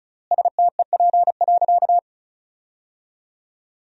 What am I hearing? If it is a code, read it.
Morse code, heard as a keyed tone, about 36 wpm, 700 Hz. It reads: STEP.
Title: 3x Word-Spacing - Sentences from Top 400 Words - 35wpm